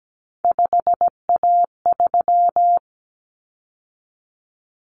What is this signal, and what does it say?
Morse code, heard as a keyed tone, about 17 wpm, 700 Hz. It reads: 5A3
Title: Common Cut Numbers - 3-Digit 17wpm